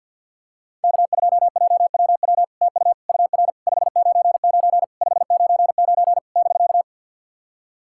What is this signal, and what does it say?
Morse code, heard as a keyed tone, about 50 wpm, 700 Hz. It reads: K2JWW TU UR 599 599 <BK>
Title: SOTA QSO Practice 50wpm